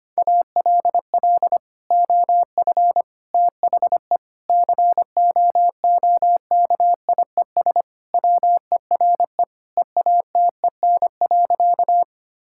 Morse code, heard as a keyed tone, about 25 wpm, 700 Hz. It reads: ALL OF THE COOKIES WERE EATEN.
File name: practice / 25wpm / Sentences with 2-Word Phrases - 25wpm